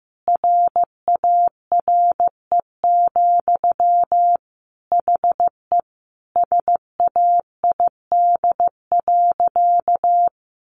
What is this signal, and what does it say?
Morse code, heard as a keyed tone, about 15 wpm, 700 Hz. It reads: RARE, HE SAID.
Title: Sentences from Top 1100 Words 15wpm